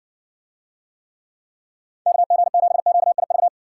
Morse code, heard as a keyed tone, about 56 wpm, 700 Hz. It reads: KG7QIV